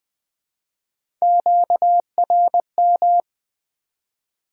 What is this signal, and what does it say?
Morse code, heard as a keyed tone, about 20 wpm, 700 Hz. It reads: QRM
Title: Top 90 QSO Elements 20wpm